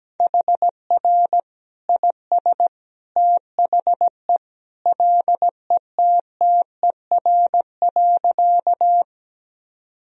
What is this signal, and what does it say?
Morse code, heard as a keyed tone, about 17 wpm, 700 Hz. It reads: HR IS THE LETTER.
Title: Sentences from Top 300 Words and Abbreviations 17wpm